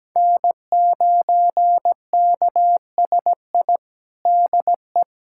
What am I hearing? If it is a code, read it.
Morse code, heard as a keyed tone, about 17 wpm, 700 Hz. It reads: N9KSI DE
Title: USA Callsign DE Callsign - 5 Characters - 17wpm